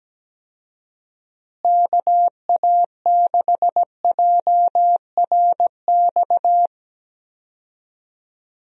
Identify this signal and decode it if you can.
Morse code, heard as a keyed tone, about 17 wpm, 700 Hz. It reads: KA6JRX
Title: USA Call Signs - 6 Characters - Repeated 3x - 17wpm